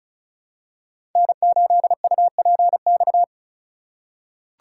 Morse code, heard as a keyed tone, about 34 wpm, 700 Hz. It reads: N8UPX
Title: USA Call Signs - 5 Characters - 4x Character Spacing - 35wpm